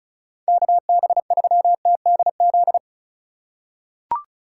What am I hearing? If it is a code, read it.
Morse code, heard as a keyed tone, about 36 wpm, 700 Hz. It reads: KB3TDZ
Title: USA Call Signs - 6 Characters - 4x Character Spacing - 35wpm